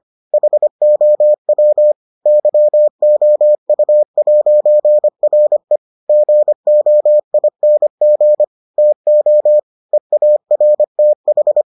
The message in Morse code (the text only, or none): HOW YOU'RE GOING TO EARTH